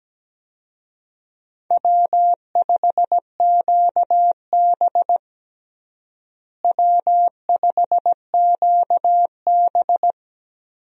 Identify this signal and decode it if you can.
Morse code, heard as a keyed tone, about 17 wpm, 700 Hz. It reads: W5QB W5QB
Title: USA Call Signs - 4 Characters - Repeated 3x - 17wpm